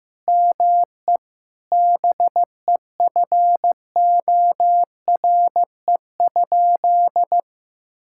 Morse code, heard as a keyed tone, about 15 wpm, 700 Hz. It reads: ME BEFORE?
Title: Sentences from Top 1000 Words - Review Part 6 - 15wpm